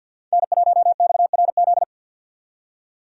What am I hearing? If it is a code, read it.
Morse code, heard as a keyed tone, about 48 wpm, 700 Hz. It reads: N1XRZ
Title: Speed Racing - USA 1x3 Call Signs 50wpm